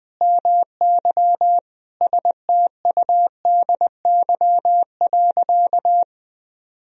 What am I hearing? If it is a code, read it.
Morse code, heard as a keyed tone, about 21 wpm, 700 Hz. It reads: MY STUDY.
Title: Sentences from Top 1000 Words - Review Part 1 - 20wpm